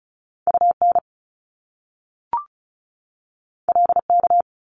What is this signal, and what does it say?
Morse code, heard as a keyed tone, about 36 wpm, 700 Hz. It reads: UN LK
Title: Top Two Letter Combinations in Words 35wpm